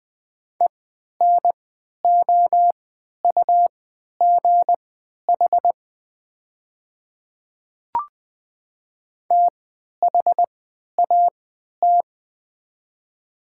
Morse code, heard as a keyed tone, about 20 wpm, spaced as for 11 wpm, 700 Hz. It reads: ENOUGH THAT
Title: 3x Character Spacing - Top 500 Words - 20wpm